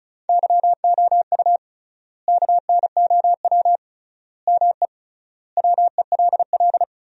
Morse code, heard as a keyed tone, about 36 wpm, 700 Hz. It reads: YOU KNOW ME WELL
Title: 3x Word-Spacing - Sentences from Top 400 Words - 35wpm